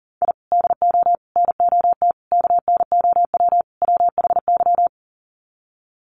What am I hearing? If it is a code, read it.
Morse code, heard as a keyed tone, about 40 wpm, 700 Hz. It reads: I DO NOT KNOW WHY
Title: Sets of 5 Words 40wpm